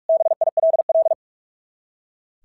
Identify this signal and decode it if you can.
Morse code, heard as a keyed tone, about 46 wpm, 690 Hz. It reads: BILL
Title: SOTA QSO Practice 45wpm